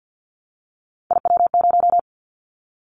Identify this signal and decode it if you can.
Morse code, heard as a keyed tone, about 50 wpm, 700 Hz. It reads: IW0
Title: Callsign Prefixes for Top 25 DXCC - 50wpm